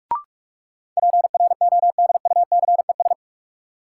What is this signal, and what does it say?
Morse code, heard as a keyed tone, about 46 wpm, 700 Hz. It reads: PRODUCES
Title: Sentences from Top 1000 Words - Review Part 6 - New Word Drill - 45wpm